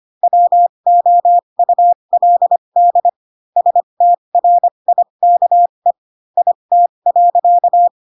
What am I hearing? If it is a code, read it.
Morse code, heard as a keyed tone, about 25 wpm, spaced as for 22 wpm, 700 Hz. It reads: WOULD STRIKE IT.
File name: Pinocchio / Pinocchio ch4 18 25 ik1hge